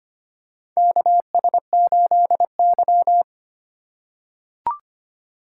Morse code, heard as a keyed tone, about 25 wpm, 700 Hz. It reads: KS8Y
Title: USA Call Signs - 4 Characters - 2x Character Spacing - 25wpm